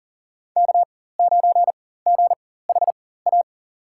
Morse code, heard as a keyed tone, about 40 wpm, spaced as for 19 wpm, 700 Hz. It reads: K9GHA
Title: USA Call Signs - 5 Characters - 4x Character Spacing - 40wpm